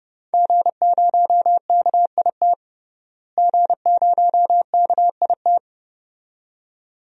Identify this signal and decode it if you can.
Morse code, heard as a keyed tone, about 30 wpm, 700 Hz. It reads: G0KIT G0KIT
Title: DXCC-100 Call Signs - 5 Characters - Repeated 2x - 30wpm